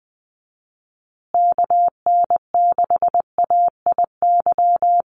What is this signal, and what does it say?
Morse code, heard as a keyed tone, about 20 wpm, 700 Hz. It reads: KN6AIY
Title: USA Call Signs Repeated 2x 20wpm